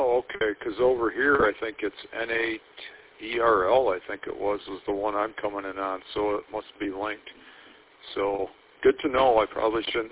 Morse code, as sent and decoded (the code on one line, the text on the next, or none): none